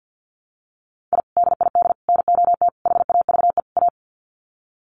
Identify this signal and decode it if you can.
Morse code, heard as a keyed tone, about 48 wpm, 700 Hz. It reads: I DID NOT HAVE A
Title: Sets of 5 Words 50wpm